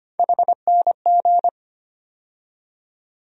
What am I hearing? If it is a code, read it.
Morse code, heard as a keyed tone, about 25 wpm, 700 Hz. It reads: HNG
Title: USA Call Sign Postfixes - 3 Letters 25wpm